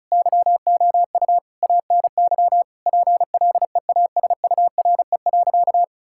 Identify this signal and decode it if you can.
Morse code, heard as a keyed tone, about 36 wpm, 700 Hz. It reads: YOU ANY PLEASURE.
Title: Sentences from Top 1600 Words 35wpm